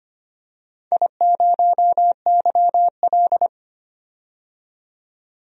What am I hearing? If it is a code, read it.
Morse code, heard as a keyed tone, about 25 wpm, 700 Hz. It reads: I0YL
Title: DXCC-100 Call Signs - 4 Characters - 25wpm